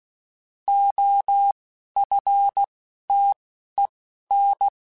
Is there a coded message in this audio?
Yes: Morse code from 0.7 to 4.7 s, heard as a keyed tone, about 16 wpm, spaced as for 11 wpm, 800 Hz. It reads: OFTEN